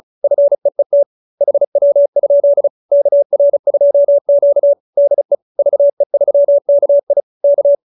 Morse code, heard as a keyed tone, about 34 wpm, 590 Hz. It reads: FEET HW? KR2Q DE VE3KI K